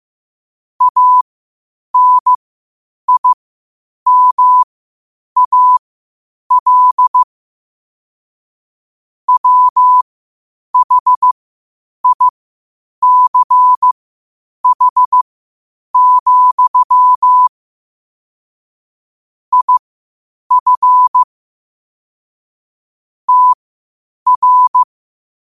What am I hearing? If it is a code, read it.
Morse code, heard as a keyed tone, about 15 wpm, spaced as for 8 wpm, 1000 Hz. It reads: ANIMAL WHICH, IF TR
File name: qotdfiles / 8wpm